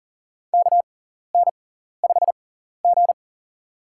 Morse code, heard as a keyed tone, about 40 wpm, spaced as for 14 wpm, 700 Hz. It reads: KN5G
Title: USA Call Signs - 6x Character Spacing - 40wpm